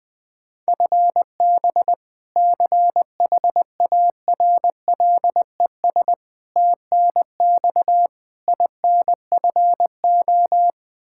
Morse code, heard as a keyed tone, about 20 wpm, 700 Hz. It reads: FB CHARLES TNX INFO
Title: Practice QSO - Exchange 3 - More Info - 20wpm